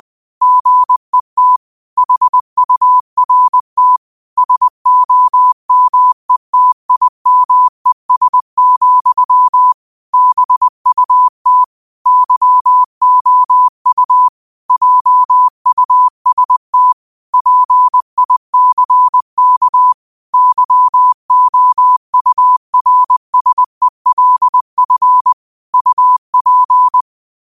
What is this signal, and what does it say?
Morse code, heard as a keyed tone, about 20 wpm, 1000 Hz. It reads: GET HURT SOMETIMES, BUT YOU JUST PICK YOURSELF UP
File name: qotdfiles / 20wpm